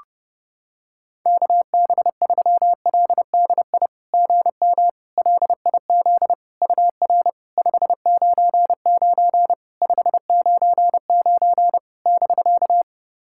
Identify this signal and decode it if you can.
Morse code, heard as a keyed tone, about 30 wpm, 690 Hz. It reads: KB3LDI GM LIZ UR 599 599 <BK>
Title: SOTA QSO Practice 30wpm